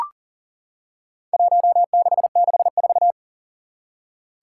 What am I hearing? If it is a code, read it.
Morse code, heard as a keyed tone, about 42 wpm, 700 Hz. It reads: 1664
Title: Four Challenging Letter-Number Combinations 40wpm